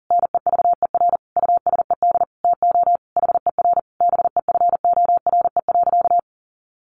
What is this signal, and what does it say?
Morse code, heard as a keyed tone, about 40 wpm, 700 Hz. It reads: NEVER USED TO HER BEFORE.
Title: Sentences from Top 600 Words 40wpm